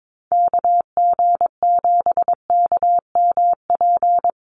Morse code, heard as a keyed tone, about 22 wpm, 700 Hz. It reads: KG7KMP
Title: USA Call Signs 22wpm